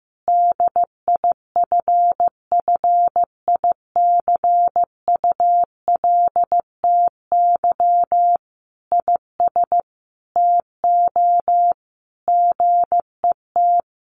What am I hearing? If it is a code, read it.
Morse code, heard as a keyed tone, about 15 wpm, 700 Hz. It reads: DIFFICULTY IS TO GET
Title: Sentences from Top 1500 Words 15wpm